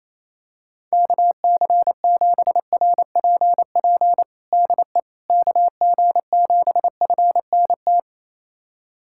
Morse code, heard as a keyed tone, about 28 wpm, 700 Hz. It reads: KC7RPP DE KG7FNT
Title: USA Callsign DE Callsign - 6 Characters - 28wpm